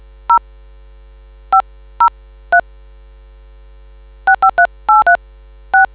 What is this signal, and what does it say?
DTMF, heard as telephone touch tones, tones of about 100 ms. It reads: 0 503 653836